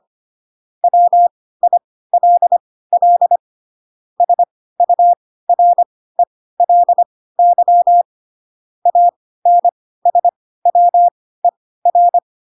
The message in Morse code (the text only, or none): WILL SURELY ANSWER